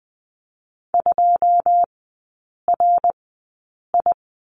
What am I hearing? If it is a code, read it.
Morse code, heard as a keyed tone, about 20 wpm, spaced as for 8 wpm, 690 Hz. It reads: 2RI
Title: 081 Lesson 14e - Number 2 - Three Characters 20wpm